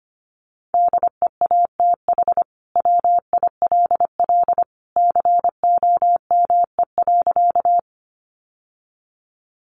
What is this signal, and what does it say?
Morse code, heard as a keyed tone, about 25 wpm, 700 Hz. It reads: DEATH WILL COME.
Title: Sentences from Top 1100 Words 25wpm